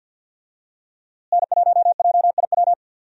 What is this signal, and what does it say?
Morse code, heard as a keyed tone, about 47 wpm, 700 Hz. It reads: N1JIW
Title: USA Call Signs - 5 Characters - Repeated 3x - 50wpm